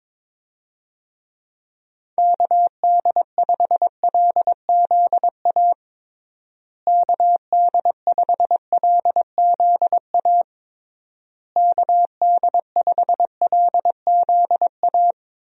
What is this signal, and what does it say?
Morse code, heard as a keyed tone, about 22 wpm, 700 Hz. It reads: KD5LZA KD5LZA KD5LZA
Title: USA Call Signs - 6 Characters - Repeated 3x - 22wpm